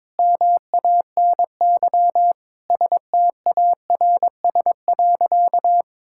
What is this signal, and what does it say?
Morse code, heard as a keyed tone, about 22 wpm, 700 Hz. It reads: MANY STARS.
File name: practice / 22wpm / Sentences with 2-Word Phrases - 22wpm